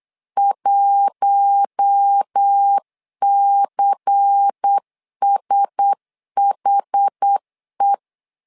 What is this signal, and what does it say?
Morse code, heard as a keyed tone, about 8 wpm, 800 Hz. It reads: 1CSHE
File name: LEZIONI / 5 LBVC / 5 LBVC 8wpm 2935